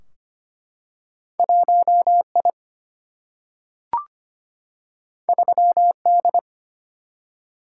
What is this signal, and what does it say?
Morse code, heard as a keyed tone, about 25 wpm, 700 Hz. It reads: 1 I 3 D
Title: Single-Letter Single-Number 25wpm